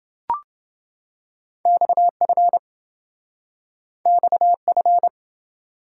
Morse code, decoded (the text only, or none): XF XF